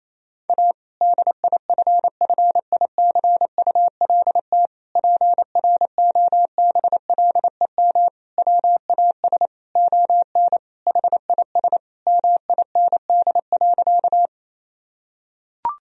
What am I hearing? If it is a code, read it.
Morse code, heard as a keyed tone, about 28 wpm, 700 Hz. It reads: A DIFFICULT PROBLEM WAS ON HIS MIND.
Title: Sentences from Top 1000 Words - Review Part 2 - 28wpm